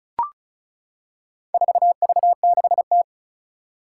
Morse code, heard as a keyed tone, about 36 wpm, 700 Hz. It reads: 4V6T
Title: Four Challenging Letter-Number Combinations 35wpm